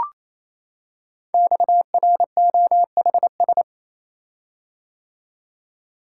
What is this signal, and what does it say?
Morse code, heard as a keyed tone, about 27 wpm, 700 Hz. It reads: XROHS
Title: Code Groups - 5 Letters - 28wpm